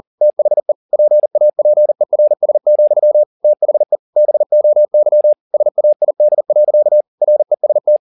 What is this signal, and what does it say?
Morse code, heard as a keyed tone, about 40 wpm, 600 Hz. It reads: THE PAPERS, THE BOY SAID. REST
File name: the old man and the sea 2 40wpm